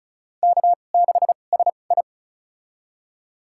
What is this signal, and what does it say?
Morse code, heard as a keyed tone, about 36 wpm, spaced as for 25 wpm, 700 Hz. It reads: K6SI
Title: USA Call Signs - 2x Character Spacing - 35wpm